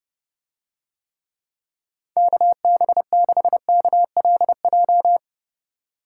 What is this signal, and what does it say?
Morse code, heard as a keyed tone, about 30 wpm, 700 Hz. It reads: KB6KLJ